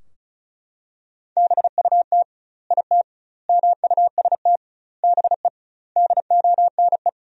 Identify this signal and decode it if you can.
Morse code, heard as a keyed tone, about 36 wpm, 700 Hz. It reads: BUT IT MUST BE DONE
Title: 2x Word-Spacing - Sentences from Top 500 Words - 35wpm